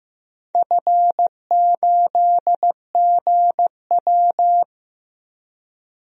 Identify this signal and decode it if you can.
Morse code, heard as a keyed tone, about 15 wpm, 700 Hz. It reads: F8GW
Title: DXCC-100 Call Signs - 4 Characters - 15wpm